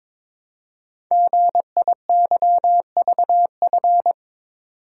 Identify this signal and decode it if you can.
Morse code, heard as a keyed tone, about 22 wpm, 700 Hz. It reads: GIYVF
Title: Code Groups - 5 Letters - 22wpm